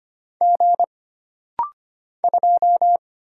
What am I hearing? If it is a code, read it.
Morse code, heard as a keyed tone, about 25 wpm, 700 Hz. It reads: G 2
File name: Single Character (Letter:Number:Punctuation:Prosign) - Rapid-Fire - 25wpm